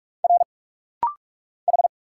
Morse code, heard as a keyed tone, about 44 wpm, 700 Hz. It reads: R H
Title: Single Letters - Rapid-Fire 45wpm